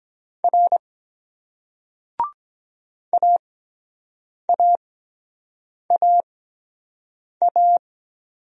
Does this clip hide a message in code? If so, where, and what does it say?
Yes: Morse code from 0.4 to 7.8 s, heard as a keyed tone, about 23 wpm, spaced as for 7 wpm, 700 Hz. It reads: R AAAA